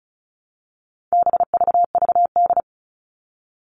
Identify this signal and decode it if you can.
Morse code, heard as a keyed tone, about 34 wpm, 700 Hz. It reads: BVVD
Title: All Cut Numbers - 4-Digit 35wpm